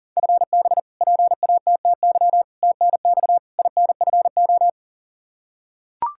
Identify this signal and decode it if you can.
Morse code, heard as a keyed tone, about 37 wpm, 700 Hz. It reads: FB PATTY TNX INFO
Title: K1USN SST QSO Practice 40wpm